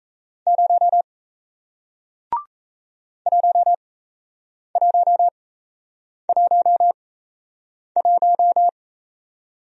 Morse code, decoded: 0 1111

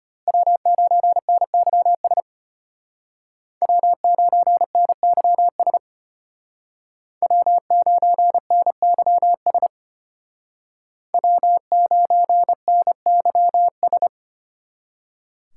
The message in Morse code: W9NYS W9NYS W9NYS W9NYS

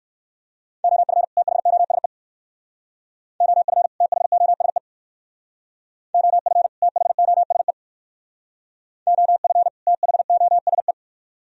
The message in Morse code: OF THOSE OF THOSE OF THOSE OF THOSE

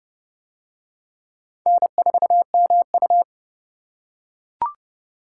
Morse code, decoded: N4MU

